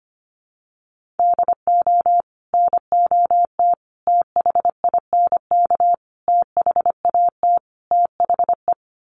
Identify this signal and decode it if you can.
Morse code, heard as a keyed tone, about 25 wpm, 700 Hz. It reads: DO NOT THINK THAT THE
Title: Sets of 5 Words Encore2 25wpm